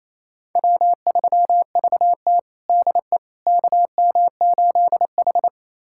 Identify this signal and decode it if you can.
Morse code, heard as a keyed tone, about 28 wpm, 700 Hz. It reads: W3VT DE KM8H
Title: USA Callsign DE Callsign - 4 Characters - 28wpm